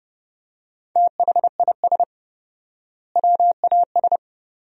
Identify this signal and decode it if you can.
Morse code, heard as a keyed tone, about 30 wpm, 700 Hz. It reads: THIS WAS
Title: 4x Word-Spacing - Sentences from Top 400 Words - 30wpm